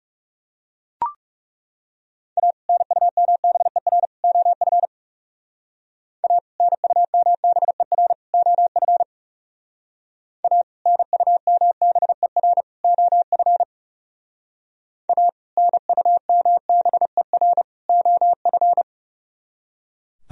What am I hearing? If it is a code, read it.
Morse code, heard as a keyed tone, about 37 wpm, 700 Hz. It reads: A NUMBER OF A NUMBER OF A NUMBER OF A NUMBER OF
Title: Speed-Racing - Sets of 3 Words 30wpm